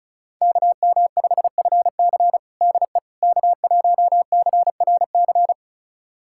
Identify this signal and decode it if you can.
Morse code, heard as a keyed tone, about 36 wpm, 700 Hz. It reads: KM5FC DE K1CRC
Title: USA Callsign DE Callsign - 5 Characters - 35wpm